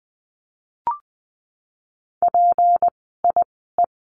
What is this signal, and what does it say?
Morse code, heard as a keyed tone, about 20 wpm, spaced as for 14 wpm, 700 Hz. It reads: PIE
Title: 153 Lesson 25f - Letter p - Words 20wpm